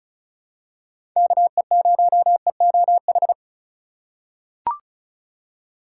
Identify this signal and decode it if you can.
Morse code, heard as a keyed tone, about 36 wpm, 700 Hz. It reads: KE0EOH
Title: USA Call Signs - 4x Character Spacing - 35wpm